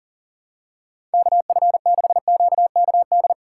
Morse code, heard as a keyed tone, about 40 wpm, 700 Hz. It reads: KF6QKD
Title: USA Call Signs - 6 Characters - 3x Character Spacing - 40wpm